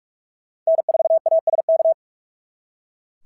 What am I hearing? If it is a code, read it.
Morse code, heard as a keyed tone, about 46 wpm, 690 Hz. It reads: N4ASK
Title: SOTA QSO Practice 45wpm